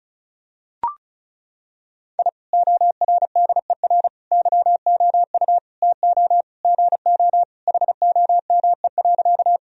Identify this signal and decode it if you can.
Morse code, heard as a keyed tone, about 36 wpm, 700 Hz. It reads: I ORDER YOU TO GO HOME.
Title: Sentences from Top 1000 Words - Review Part 2 - 35wpm